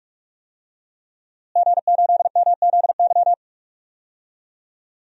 Morse code, heard as a keyed tone, about 46 wpm, 700 Hz. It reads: G8MZY
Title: DXCC-100 Call Signs - 5 Characters - Repeated 2x - 45wpm